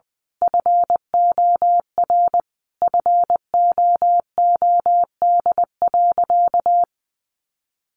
Morse code, heard as a keyed tone, about 20 wpm, 700 Hz. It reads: FOR FOOD.
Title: A Sound Of Thunder 20wpm